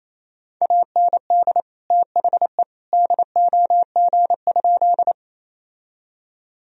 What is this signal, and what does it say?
Morse code, heard as a keyed tone, about 28 wpm, 700 Hz. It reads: AND THE DOG?
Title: Sentences from Top 1000 Words - Review Part 2 - 28wpm